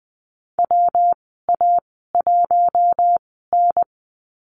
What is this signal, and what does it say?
Morse code, heard as a keyed tone, about 20 wpm, spaced as for 14 wpm, 700 Hz. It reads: WA1N
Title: 122 Lesson 20g - Number 3 - Call Signs 20wpm